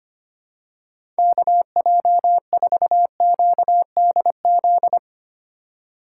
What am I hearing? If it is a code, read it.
Morse code, heard as a keyed tone, about 25 wpm, 700 Hz. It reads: KJ4QDZ
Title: USA Call Signs - 6 Characters - 3x Character Spacing - 25wpm